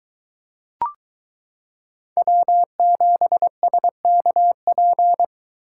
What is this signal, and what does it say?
Morse code, heard as a keyed tone, about 22 wpm, 700 Hz. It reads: W7SKP